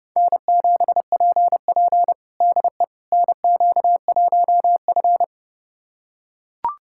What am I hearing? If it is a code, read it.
Morse code, heard as a keyed tone, about 32 wpm, 700 Hz. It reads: N7PP DE NQ1F
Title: USA Callsign DE Callsign - 4 Characters - 30wpm